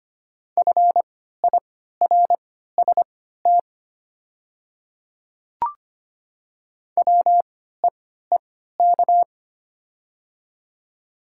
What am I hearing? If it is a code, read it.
Morse code, heard as a keyed tone, about 25 wpm, spaced as for 14 wpm, 700 Hz. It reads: FIRST WEEK
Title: 3x Character Spacing - Top 500 Words - 25wpm